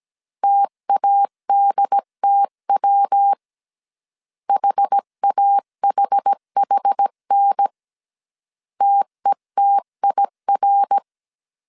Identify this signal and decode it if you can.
Morse code, heard as a keyed tone, about 17 wpm, 800 Hz. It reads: TADTW HAHHN TETIR
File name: CW / DET / 1 3 1500C S A 16wpm 2141